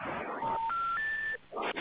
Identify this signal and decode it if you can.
Special information tone, with noise.